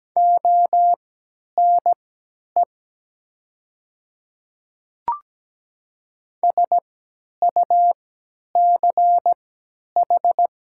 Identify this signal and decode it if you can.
Morse code, heard as a keyed tone, about 17 wpm, spaced as for 10 wpm, 700 Hz. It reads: ONE SUCH
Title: 3x Character Spacing - Top 200 Words - 17wpm